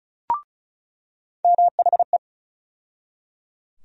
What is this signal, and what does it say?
Morse code, heard as a keyed tone, about 36 wpm, 700 Hz. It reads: MHE
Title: USA Call Sign Postfixes - 3 Letters 35wpm